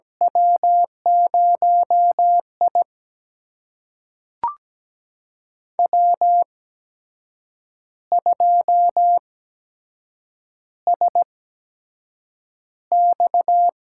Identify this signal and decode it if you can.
Morse code, heard as a keyed tone, about 17 wpm, spaced as for 7 wpm, 700 Hz. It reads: W0I W2SX
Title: USA Call Signs - 8x Character Spacing - 17wpm